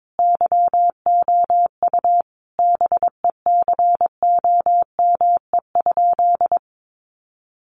Morse code, heard as a keyed tone, about 22 wpm, 700 Hz. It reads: YOU BECOME?
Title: Sentences from Top 600 Words 22wpm